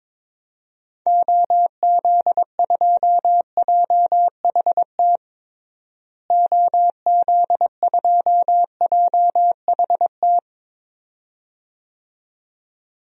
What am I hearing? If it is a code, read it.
Morse code, heard as a keyed tone, about 22 wpm, 700 Hz. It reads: OZ2JHT OZ2JHT
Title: DXCC-100 Call Signs - 6 Characters - Repeated 2x - 22wpm